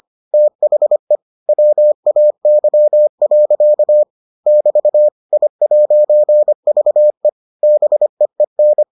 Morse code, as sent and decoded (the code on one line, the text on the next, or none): - .... / .-- .- -.-- .-.-.- / -...- / .. .----. ...- . / -... . -.
THE WAY. <BT> I'VE BEEN